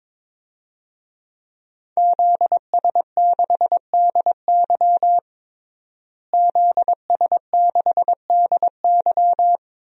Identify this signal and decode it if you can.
Morse code, heard as a keyed tone, about 22 wpm, 700 Hz. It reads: ZS6DY ZS6DY
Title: DXCC-100 Call Signs - 5 Characters - Repeated 2x - 22wpm